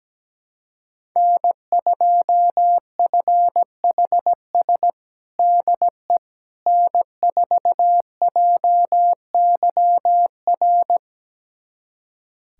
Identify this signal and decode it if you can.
Morse code, heard as a keyed tone, about 17 wpm, 700 Hz. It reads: N2FHS DE N4JYR
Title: USA Callsign DE Callsign - 5 Characters - 17wpm